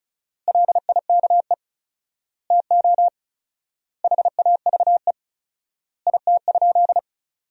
Morse code, heard as a keyed tone, about 36 wpm, 700 Hz. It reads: LIKE TO HAVE IT?
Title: 4x Word-Spacing - Sentences from Top 100 Words - 35wpm